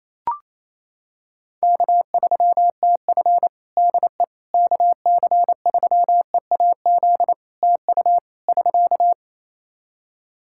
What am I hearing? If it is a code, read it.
Morse code, heard as a keyed tone, about 28 wpm, 700 Hz. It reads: K3TF DE KC3EAZ TU <SK>
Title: Practice QSO - Exchange 4 - Ending - 28wpm